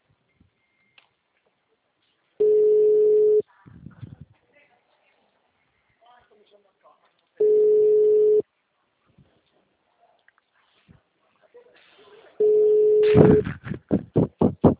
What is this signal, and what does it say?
Ringback tone (Europe, 425 Hz), with speech.
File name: out- -2004- - - .52